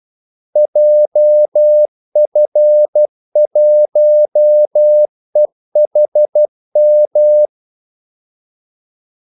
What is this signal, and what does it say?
Morse code, heard as a keyed tone, about 12 wpm, 600 Hz. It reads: JF1EHM
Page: 2 ekstra ordmellemrum mellem kaldesignalerne